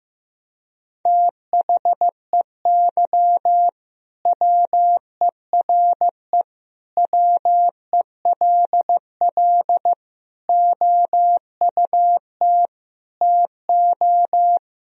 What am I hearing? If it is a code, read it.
Morse code, heard as a keyed tone, about 15 wpm, 700 Hz. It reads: THEY WERE WELL OUT TO
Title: Sentences from Top 1000 Words - Review Part 3 - 15wpm